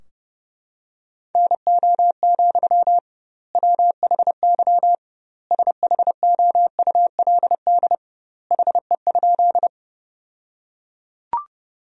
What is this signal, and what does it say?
Morse code, heard as a keyed tone, about 30 wpm, 700 Hz. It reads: NO, WHY SHOULD HE?
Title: 2x Word-Spacing - Sentences from Top 300 Words - 30wpm